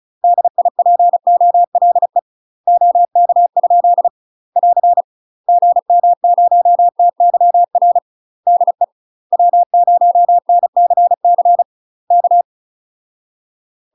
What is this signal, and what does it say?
Morse code, heard as a keyed tone, about 36 wpm, 700 Hz. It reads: DIPOLE OK? <AR> GM0TYR DE W0NCC K
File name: UK MASTER QSOs TRAINING / 35WPM / UK MASTER QSOs 0014